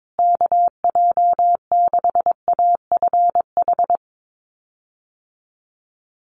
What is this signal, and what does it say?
Morse code, heard as a keyed tone, about 22 wpm, 690 Hz. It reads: KJ6AFH
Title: USA Call Signs Repeated 3x 22wpm